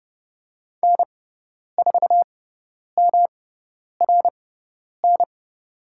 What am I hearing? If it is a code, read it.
Morse code, heard as a keyed tone, about 31 wpm, spaced as for 10 wpm, 700 Hz. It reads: N4MRN
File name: Lesson 18g - Letter m - Call Signs 30wpm